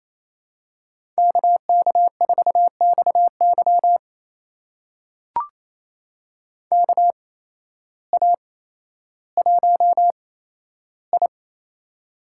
Morse code, heard as a keyed tone, about 28 wpm, spaced as for 13 wpm, 700 Hz. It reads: KK4XY KA1I
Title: USA Call Signs - 8x Character Spacing - 28wpm